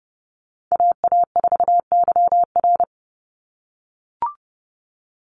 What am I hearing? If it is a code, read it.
Morse code, heard as a keyed tone, about 30 wpm, 700 Hz. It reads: AA4YR
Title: USA Call Signs 30wpm